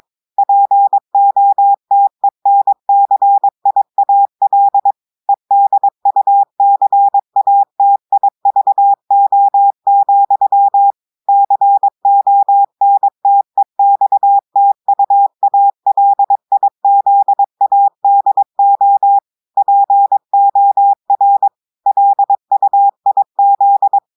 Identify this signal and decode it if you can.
Morse code, heard as a keyed tone, about 22 wpm, 800 Hz. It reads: POTENCIAL EDUCATIVO, CONTEXTUALIZADO POR LUIZ
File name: audios / portugais / portugais texte en clair 1 vitesse 22 mots minutes